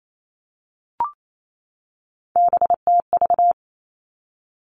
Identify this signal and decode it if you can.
Morse code, heard as a keyed tone, about 28 wpm, 700 Hz. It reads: BTV
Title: Three Challenging Letter-Number Combinations 28wpm